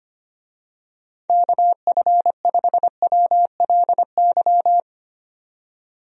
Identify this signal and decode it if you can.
Morse code, heard as a keyed tone, about 25 wpm, 700 Hz. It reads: KF5WLY
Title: USA Call Signs - 6 Characters - 25wpm